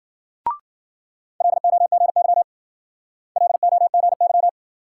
Morse code, heard as a keyed tone, about 48 wpm, 700 Hz. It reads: LOGY LOGY
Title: Speed-Racing Multi-letter Phonemes - 40wpm